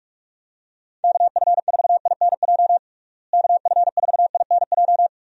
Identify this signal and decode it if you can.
Morse code, heard as a keyed tone, about 41 wpm, 700 Hz. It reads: KF4INJ KF4INJ
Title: USA Call Signs - 6 Characters - Repeated 3x - 45wpm